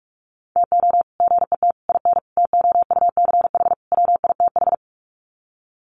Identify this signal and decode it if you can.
Morse code, heard as a keyed tone, about 44 wpm, 700 Hz. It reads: TO GET IN TOUCH WITH
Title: Sets of 5 Words Encore2 45wpm